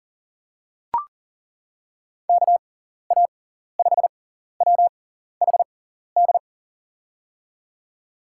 Morse code, heard as a keyed tone, about 40 wpm, spaced as for 14 wpm, 700 Hz. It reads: KA5WHD
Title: USA Call Signs - 6 Characters - 6x Character Spacing - 40wpm